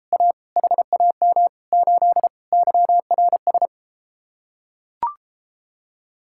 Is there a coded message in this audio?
Yes: Morse code from 0.1 to 3.7 s, heard as a keyed tone, about 33 wpm, 700 Hz. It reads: A HAM 8 YRS